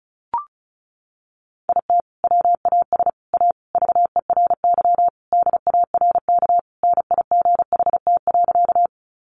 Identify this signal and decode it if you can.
Morse code, heard as a keyed tone, about 34 wpm, 700 Hz. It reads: IT WAS A VERY DARK NIGHT.
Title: Sentences from Top 500 Words 35wpm